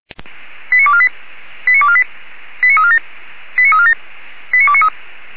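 Five-tone selective call (ZVEI), heard as quick runs of five tones, tones of about 70 ms. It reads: 89248 89248 89347 89347 89293
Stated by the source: Five-tone ZVEI selective call received by SDR.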